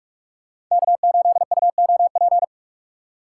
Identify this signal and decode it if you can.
Morse code, heard as a keyed tone, about 44 wpm, 700 Hz. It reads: K8UOP